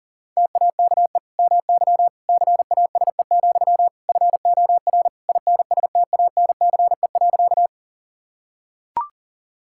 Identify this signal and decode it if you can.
Morse code, heard as a keyed tone, about 40 wpm, 700 Hz. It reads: TAKE MY CASE, FOR INSTANCE.
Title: Sentences from Top 1000 Words - Review Part 5 - 40wpm